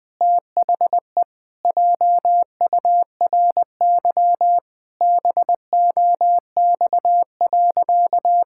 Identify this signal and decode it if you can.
Morse code, heard as a keyed tone, about 20 wpm, 700 Hz. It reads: THE JURY BOX.
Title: Pangram Challenge v2 - 20wpm